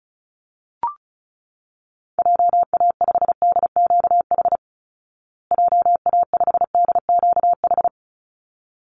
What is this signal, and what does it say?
Morse code, heard as a keyed tone, about 34 wpm, 700 Hz. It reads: JA5DQH JA5DQH
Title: Call Signs Repeated 3x - 35wpm